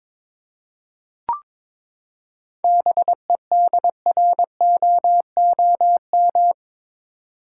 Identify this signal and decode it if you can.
Morse code, heard as a keyed tone, about 22 wpm, 700 Hz. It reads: BEDROOM
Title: Seven Letter Words - 22wpm